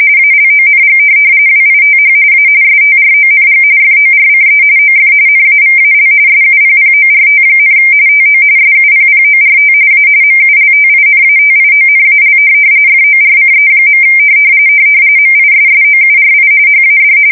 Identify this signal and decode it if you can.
RTTY, about 45.45 baud. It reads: ALONG WITH A STRONG FORCE OF NAVAL SUPPORT, TO SUPPORT A MULTINATIONAL FORCE. MISSOURI HAD
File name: fsk170 / USS-Missouri